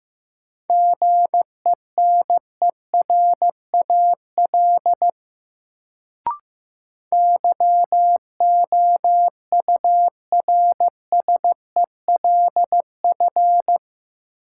Morse code, heard as a keyed tone, about 15 wpm, 700 Hz. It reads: GENERAL YOURSELF
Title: Sentences from Top 600 Words - New Word Drill - 15wpm